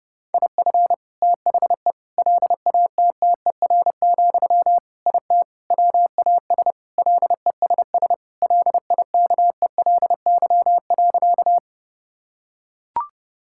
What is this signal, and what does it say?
Morse code, heard as a keyed tone, about 30 wpm, 700 Hz. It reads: IF THE LATTER, IT WAS LESS LIKELY.
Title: Sentences from Top 1600 Words 30wpm